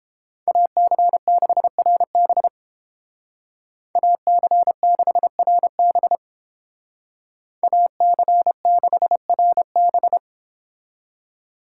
Morse code, heard as a keyed tone, about 31 wpm, 700 Hz. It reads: AC6RB AC6RB AC6RB